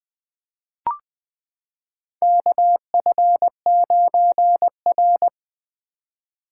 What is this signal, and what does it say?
Morse code, heard as a keyed tone, about 20 wpm, 700 Hz. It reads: KF9R
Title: 2023 POTA Hunter Call Signs - 20wpm